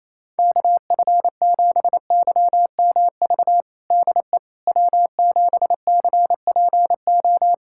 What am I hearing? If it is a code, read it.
Morse code, heard as a keyed tone, about 27 wpm, 700 Hz. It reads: KF7YMV DE W7CPO
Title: USA Callsign DE Callsign 28wpm